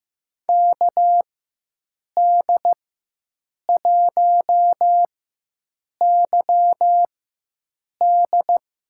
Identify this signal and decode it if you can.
Morse code, heard as a keyed tone, about 15 wpm, spaced as for 7 wpm, 700 Hz. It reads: KD1YD
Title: USA Call Signs - 5 Characters - 4x Character Spacing - 15wpm